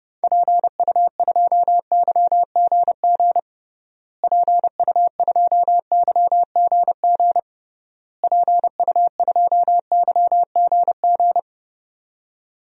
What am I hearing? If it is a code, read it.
Morse code, heard as a keyed tone, about 30 wpm, 700 Hz. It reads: PU2YGG PU2YGG PU2YGG
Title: DXCC-100 Call Signs - 6 Characters - Repeated 3x - 30wpm